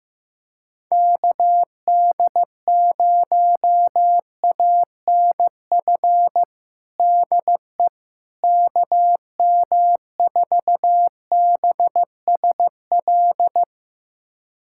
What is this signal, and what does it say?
Morse code, heard as a keyed tone, about 15 wpm, 700 Hz. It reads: KD0ANF DE KM4BSL
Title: USA Callsign DE Callsign - 6 Characters - 15wpm